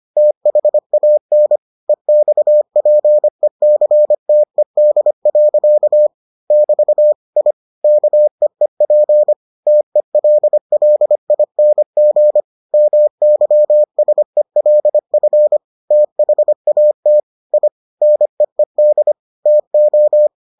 Morse code, heard as a keyed tone, about 25 wpm, 600 Hz. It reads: THAN EXPECTED. <BT> I KEEP TELLING MYSELF THAT I NEED TO